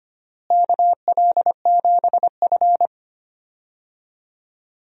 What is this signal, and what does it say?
Morse code, heard as a keyed tone, about 25 wpm, 700 Hz. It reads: KL7F
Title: DXCC-100 Call Signs - 4 Characters - Repeated 3x - 25wpm